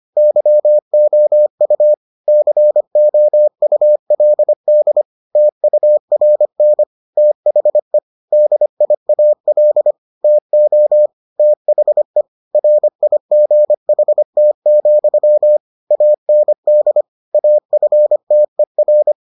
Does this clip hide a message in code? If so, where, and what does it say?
Yes: Morse code from 0.2 to 19.1 s, heard as a keyed tone, about 25 wpm, 600 Hz. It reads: YOU COULD TURN THE DIAL TO THE RIGHT, AND AFTER